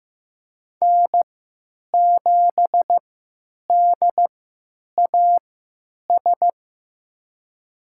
Morse code, heard as a keyed tone, about 15 wpm, spaced as for 9 wpm, 700 Hz. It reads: N7DAS
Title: USA Call Signs - 3x Character Spacing - 15wpm